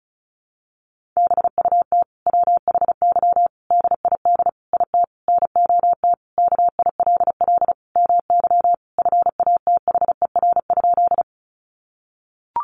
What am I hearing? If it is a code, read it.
Morse code, heard as a keyed tone, about 34 wpm, 700 Hz. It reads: BUT WHY DID IT NOT KILL MY FATHER?
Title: Sentences from Top 1400 Words 35wpm